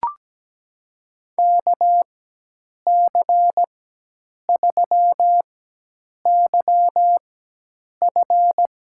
Morse code, heard as a keyed tone, about 17 wpm, spaced as for 8 wpm, 700 Hz. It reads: KC3YF
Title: USA Call Signs - 5 Characters - 4x Character Spacing - 17wpm